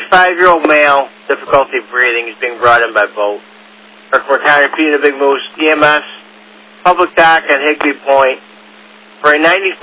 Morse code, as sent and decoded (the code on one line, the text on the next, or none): none